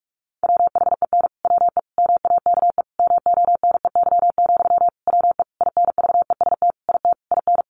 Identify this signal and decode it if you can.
Morse code, heard as a keyed tone, about 44 wpm, 700 Hz. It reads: WHEN WE MAKE MONEY, WE INVEST IT IN
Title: Sentences from Top 1100 Words 45wpm